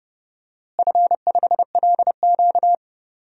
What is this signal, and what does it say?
Morse code, heard as a keyed tone, about 30 wpm, 700 Hz. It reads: F5LQ
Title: DXCC-100 Call Signs - 4 Characters - Repeated 3x - 30wpm